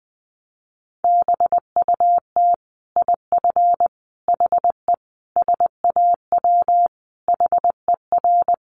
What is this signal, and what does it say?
Morse code, heard as a keyed tone, about 20 wpm, 700 Hz. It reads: BUT IF HE SAW HER
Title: Sentences from Top 1400 Words 20wpm